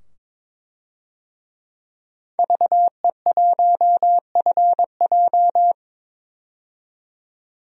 Morse code, heard as a keyed tone, about 22 wpm, 700 Hz. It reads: VE1FJ
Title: DXCC-100 Call Signs - 5 Characters - 22wpm